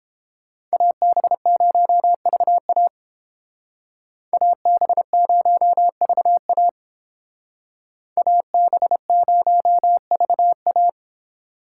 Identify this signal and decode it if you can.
Morse code, heard as a keyed tone, about 29 wpm, 700 Hz. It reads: AB0VA AB0VA AB0VA